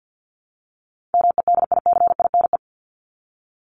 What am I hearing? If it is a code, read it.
Morse code, heard as a keyed tone, about 50 wpm, 700 Hz. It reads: MEDICINE
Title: Top 751-1000 Words 50wpm